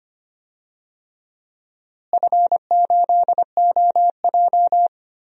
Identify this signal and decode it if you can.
Morse code, heard as a keyed tone, about 25 wpm, 700 Hz. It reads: F8OJ
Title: DXCC-100 Call Signs - 4 Characters - Repeated 3x - 25wpm